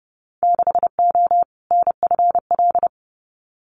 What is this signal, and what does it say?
Morse code, heard as a keyed tone, about 30 wpm, 700 Hz. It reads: 6O NFL
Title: Winter Field Day Exchanges 30wpm